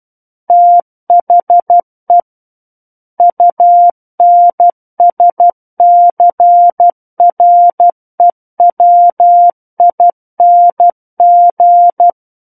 Morse code, heard as a keyed tone, about 12 wpm, 700 Hz. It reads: THE UNSCREWING
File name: Book One / 03-Chapter 3 - 12 WPM